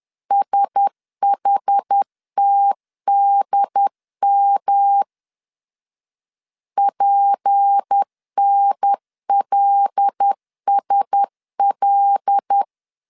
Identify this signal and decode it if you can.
Morse code, heard as a keyed tone, about 10 wpm, 800 Hz. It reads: SHTDM PNLSL